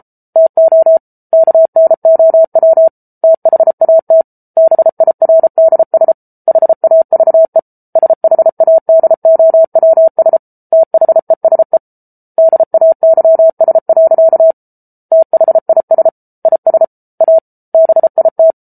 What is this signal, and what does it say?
Morse code, heard as a keyed tone, about 34 wpm, 650 Hz. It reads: TO KNOW THAT BIRDS HAVE SHADOWS THESE DAYS. THIS IS A BIT